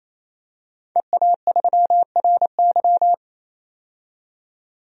Morse code, heard as a keyed tone, about 28 wpm, 700 Hz. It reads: EA3RY
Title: DXCC-100 Call Signs - 5 Characters - 28wpm